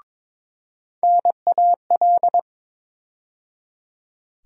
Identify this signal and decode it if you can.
Morse code, heard as a keyed tone, about 22 wpm, 700 Hz. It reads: NAL
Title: USA Call Sign Postfixes - 3 Letters 22wpm